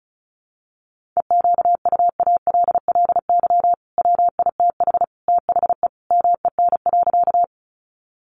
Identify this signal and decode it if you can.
Morse code, heard as a keyed tone, about 34 wpm, 700 Hz. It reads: EQUALLY WITH THE MEN.
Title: Sentences from Top 1400 Words 35wpm